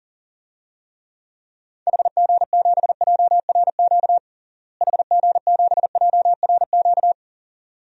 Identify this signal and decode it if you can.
Morse code, heard as a keyed tone, about 40 wpm, 700 Hz. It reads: HG7JRQ HG7JRQ
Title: DXCC-100 Call Signs - 6 Characters - Repeated 2x - 40wpm